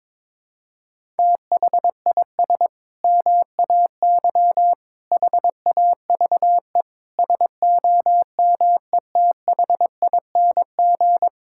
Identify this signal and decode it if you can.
Morse code, heard as a keyed tone, about 22 wpm, 700 Hz. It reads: THIS MAY HAVE SOMETHING
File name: practice / 22wpm / Sentences with 8-Word Phrases - 22wpm